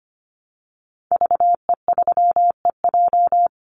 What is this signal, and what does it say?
Morse code, heard as a keyed tone, about 25 wpm, 700 Hz. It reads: VE3EJ
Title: Call Signs Repeated 3x - 25wpm